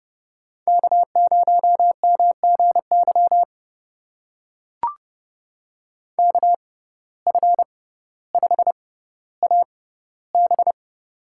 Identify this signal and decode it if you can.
Morse code, heard as a keyed tone, about 31 wpm, spaced as for 15 wpm, 700 Hz. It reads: K0MGY KF5AB
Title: USA Call Signs - 5 Characters - 6x Character Spacing - 30wpm